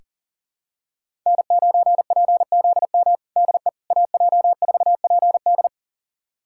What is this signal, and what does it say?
Morse code, heard as a keyed tone, about 37 wpm, 700 Hz. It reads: N9PZM DE AJ4PD
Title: USA Callsign DE Callsign - 5 Characters - 40wpm